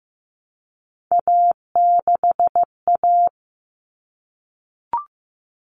Morse code, heard as a keyed tone, about 15 wpm, 700 Hz. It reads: A6A
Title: Common Cut Numbers - 3-Digit 15wpm